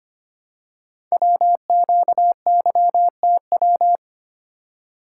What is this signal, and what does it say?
Morse code, heard as a keyed tone, about 25 wpm, 700 Hz. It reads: WQYTW
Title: Code Groups - 5 Letters - 25wpm